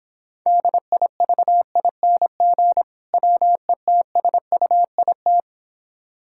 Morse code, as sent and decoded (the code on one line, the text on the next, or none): -.. .. ...- .. -. --. / .-- . - ... ..- .. -
DIVING WETSUIT